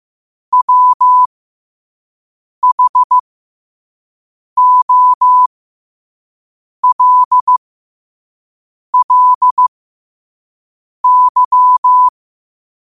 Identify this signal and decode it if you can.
Morse code, heard as a keyed tone, about 15 wpm, spaced as for 5 wpm, 1000 Hz. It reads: WHOLLY
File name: qotdfiles / 5wpm